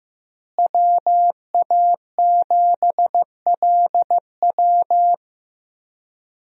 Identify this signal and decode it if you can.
Morse code, heard as a keyed tone, about 15 wpm, 700 Hz. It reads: WA7LW
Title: USA Call Signs - 5 Characters - Repeated 3x - 15wpm